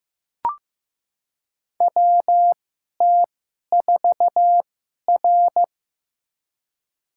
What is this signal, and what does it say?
Morse code, heard as a keyed tone, about 15 wpm, spaced as for 11 wpm, 690 Hz. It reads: WT4R
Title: USA Call Signs - 4 Characters - 2x Character Spacing - 15wpm